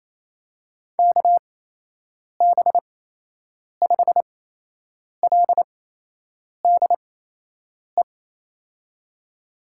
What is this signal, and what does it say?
Morse code, heard as a keyed tone, about 28 wpm, spaced as for 8 wpm, 700 Hz. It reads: KB5LDE
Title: USA Call Signs - 6 Characters - 8x Character Spacing - 28wpm